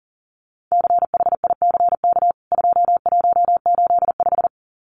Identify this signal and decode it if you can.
Morse code, heard as a keyed tone, about 40 wpm, 700 Hz. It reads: CHICK 2185
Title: CWT Exchange 40wpm